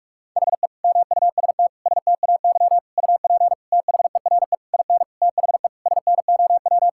Morse code, heard as a keyed tone, about 46 wpm, 700 Hz. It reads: HE MUST STAY UP THERE IN THE SNOW
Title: Sentences from Top 1600 Words 45wpm